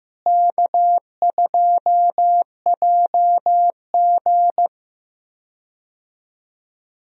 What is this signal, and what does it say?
Morse code, heard as a keyed tone, about 15 wpm, 700 Hz. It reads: K2JG
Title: USA Call Signs - 4 Characters - Repeated 2x - 15wpm